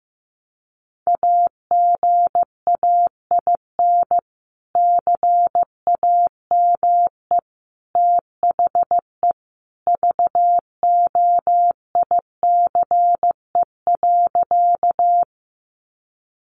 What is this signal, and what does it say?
Morse code, heard as a keyed tone, about 15 wpm, 700 Hz. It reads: AGAIN CAME THE VOICE.
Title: Sentences from Top 500 Words 15wpm